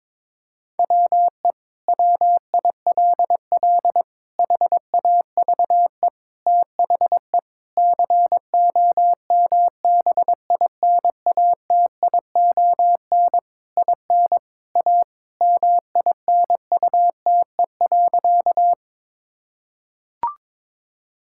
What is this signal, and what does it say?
Morse code, heard as a keyed tone, about 22 wpm, 700 Hz. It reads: WE WILL HAVE THE COMBINATION IN A MINUTE.
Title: Sentences from Top 1000 Words - Review Part 6 - 22wpm